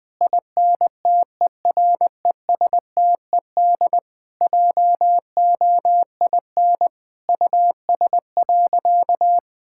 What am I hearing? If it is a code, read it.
Morse code, heard as a keyed tone, about 20 wpm, 700 Hz. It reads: INTERESTED JOIN US.
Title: Sentences with 6-Word Phrases - Spoken Beginning - 20wpm